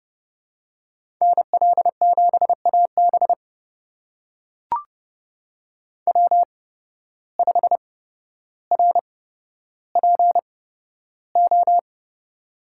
Morse code, heard as a keyed tone, about 30 wpm, spaced as for 13 wpm, 700 Hz. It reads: NL7AB W5RPO
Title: USA Call Signs - 5 Characters - 8x Character Spacing - 30wpm